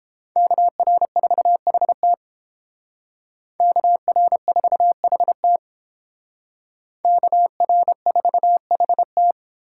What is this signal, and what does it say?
Morse code, heard as a keyed tone, about 29 wpm, 700 Hz. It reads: KR4HT KR4HT KR4HT